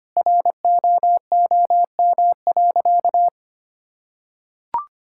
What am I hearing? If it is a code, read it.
Morse code, heard as a keyed tone, about 25 wpm, 700 Hz. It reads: ROOM.
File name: practice / 25wpm / Sentences with 3-Word Phrases - 25wpm